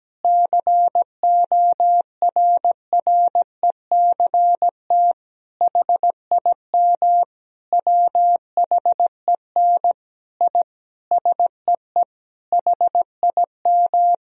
Morse code, heard as a keyed tone, about 17 wpm, 700 Hz. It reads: CORRECT HIM WHEN I SEE HIM
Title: Sentences from Top 1000 Words - Review Part 2 - 17wpm